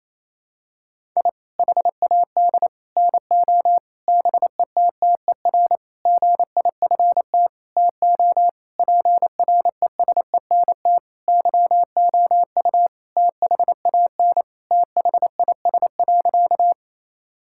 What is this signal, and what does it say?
Morse code, heard as a keyed tone, about 27 wpm, 700 Hz. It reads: I HAD NO BETTER GIFT TO PRESENT YOU THAN THIS.
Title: Sentences from Top 1000 Words - Review Part 6 - 28wpm